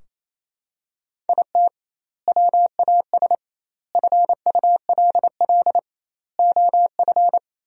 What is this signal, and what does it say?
Morse code, heard as a keyed tone, about 28 wpm, 700 Hz. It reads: IT WAS FULL OF
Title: 2x Word-Spacing - Sentences from Top 500 Words - 28wpm